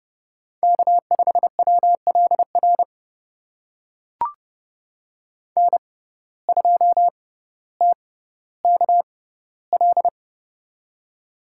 Morse code, heard as a keyed tone, about 30 wpm, spaced as for 15 wpm, 700 Hz. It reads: K5WLR N2TKL
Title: USA Call Signs - 5 Characters - 6x Character Spacing - 30wpm